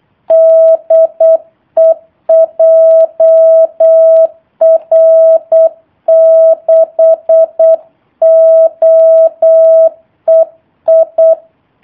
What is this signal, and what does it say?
Morse code, heard as a keyed tone, about 8 wpm, 650 Hz. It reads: DE JR6OEI
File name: jh8cbh / 2019cw / sp03- w